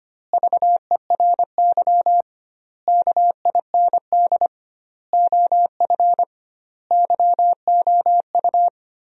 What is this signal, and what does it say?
Morse code, heard as a keyed tone, about 25 wpm, 700 Hz. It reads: VERY KIND OF YOU
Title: 2x Word-Spacing - Sentences from Top 300 Words - 25wpm